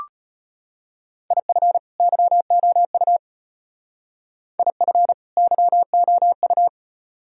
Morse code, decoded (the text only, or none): IF YOU IF YOU